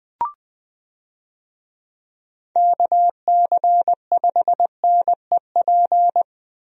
Morse code, heard as a keyed tone, about 20 wpm, 700 Hz. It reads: KC5NEP
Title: USA Call Signs - 6 Characters - Repeated 2x - 20wpm